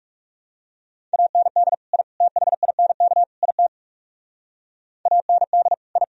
Morse code, heard as a keyed tone, about 37 wpm, 700 Hz. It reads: AND I THINK IT AND I
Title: Speed-Racing - Sets of 4 Words 30wpm